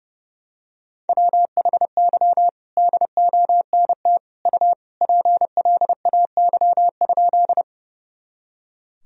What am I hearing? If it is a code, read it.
Morse code, heard as a keyed tone, about 30 wpm, 700 Hz. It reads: WHY DONT U PLAY?
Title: Sentences from Top 300 Words and Abbreviations 30wpm